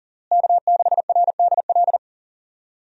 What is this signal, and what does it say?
Morse code, heard as a keyed tone, about 40 wpm, 700 Hz. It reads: K6RDL
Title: USA Call Signs - 5 Characters - Repeated 3x - 40wpm